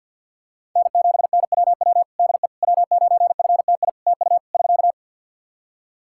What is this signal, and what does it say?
Morse code, heard as a keyed tone, about 50 wpm, 700 Hz. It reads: N7NWW DE W9FTI TU <SK>
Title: Practice QSO - Exchange 4 - Ending - 50wpm